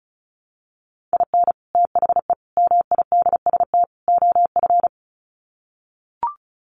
Morse code, heard as a keyed tone, about 34 wpm, 700 Hz. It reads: IN THE MIDST OF
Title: Sets of 4 Words 35wpm